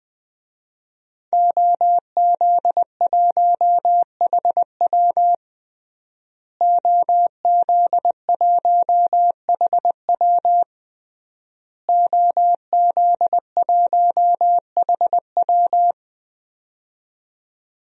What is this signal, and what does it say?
Morse code, heard as a keyed tone, about 20 wpm, 700 Hz. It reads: OZ1HW OZ1HW OZ1HW
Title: DXCC-100 Call Signs - 5 Characters - Repeated 3x - 20wpm